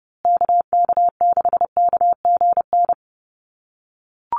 Morse code, heard as a keyed tone, about 30 wpm, 700 Hz. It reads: KK6KGN
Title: USA Call Signs Repeated 3x 30wpm